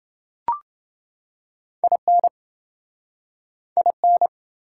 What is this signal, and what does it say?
Morse code, heard as a keyed tone, about 28 wpm, 700 Hz. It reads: IN IN